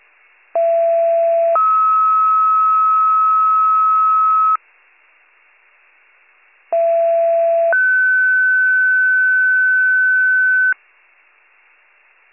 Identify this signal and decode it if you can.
Two-tone page, heard as a one-second tone, then a three-second tone, about 670 then 1290 Hz, about 670 then 1500 Hz. It is heard from 0.5 to 10.7 s.